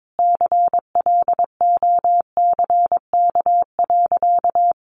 Morse code, heard as a keyed tone, about 22 wpm, 700 Hz. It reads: CLOCK.
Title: Easy Phrases 22wpm